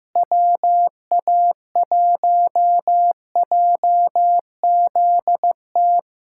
Morse code, heard as a keyed tone, about 16 wpm, 700 Hz. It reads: WA1JZT
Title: USA Callsign DE Callsign 15wpm